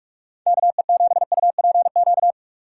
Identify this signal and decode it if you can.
Morse code, heard as a keyed tone, about 44 wpm, 700 Hz. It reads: KE7UPQ